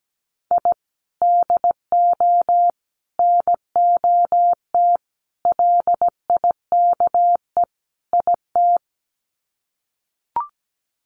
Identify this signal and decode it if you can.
Morse code, heard as a keyed tone, about 17 wpm, 700 Hz. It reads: I DO NOT LIKE IT
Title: Sets of 5 Words Encore 17wpm